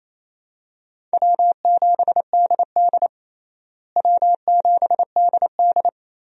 Morse code, heard as a keyed tone, about 28 wpm, 700 Hz. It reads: W7DD W7DD
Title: USA Call Signs - 4 Characters - Repeated 3x - 28wpm